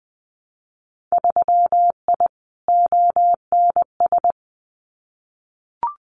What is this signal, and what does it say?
Morse code, heard as a keyed tone, about 20 wpm, 700 Hz. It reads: 3I ONS
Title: Winter Field Day Exchanges Encore 20wpm